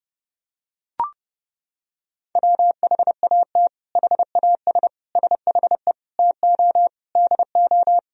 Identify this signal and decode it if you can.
Morse code, heard as a keyed tone, about 30 wpm, 700 Hz. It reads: WHAT HAS SHE TO DO
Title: Sentences from Top 100 Words and Abbreviations 30wpm